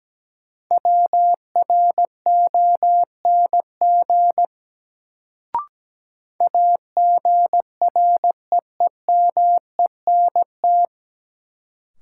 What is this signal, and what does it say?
Morse code, heard as a keyed tone, about 17 wpm, 700 Hz. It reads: WRONG AGREEMENT
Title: Sets of 5 Words Encore - New Word Drill - 17wpm